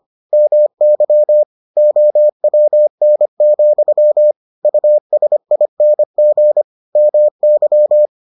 Morse code, heard as a keyed tone, about 25 wpm, 600 Hz. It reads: MY OWN, USING MY